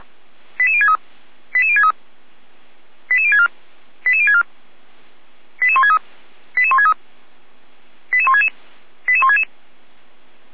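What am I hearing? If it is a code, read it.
Five-tone selective call (ZVEI), heard as quick runs of five tones, tones of about 80 ms. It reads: 88063 88063 88064 88064 88163 88163 88166 88166
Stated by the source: ZVEI five-tone call received by SDR.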